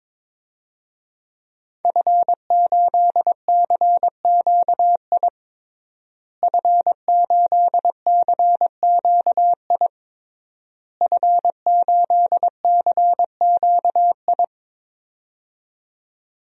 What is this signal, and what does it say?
Morse code, heard as a keyed tone, about 22 wpm, 700 Hz. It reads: F8CQI F8CQI F8CQI
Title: DXCC-100 Call Signs - 5 Characters - Repeated 3x - 22wpm